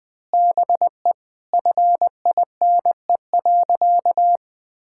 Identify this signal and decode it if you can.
Morse code, heard as a keyed tone, about 20 wpm, 700 Hz. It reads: BE FINE.
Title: Sentences with 7-Word Phrases - 20wpm